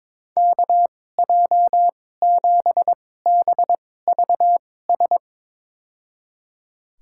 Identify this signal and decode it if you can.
Morse code, heard as a keyed tone, about 22 wpm, spaced as for 16 wpm, 700 Hz. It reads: KJ7BVS
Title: USA Call Signs - 6 Characters - 2x Character Spacing - 22wpm